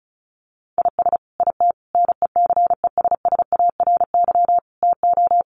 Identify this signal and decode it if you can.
Morse code, heard as a keyed tone, about 34 wpm, 700 Hz. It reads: IS IT NECESSARY TO
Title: Sentences from Top 1500 Words 35wpm